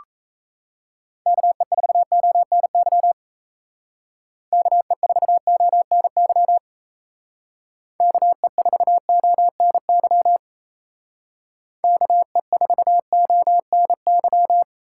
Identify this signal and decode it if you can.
Morse code, heard as a keyed tone, about 37 wpm, 700 Hz. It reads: KE4ONY KE4ONY KE4ONY KE4ONY